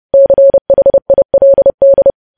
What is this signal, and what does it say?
Morse code, heard as a keyed tone, about 30 wpm, 550 Hz. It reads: CHILD
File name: podcast / News headlines-1x-30x30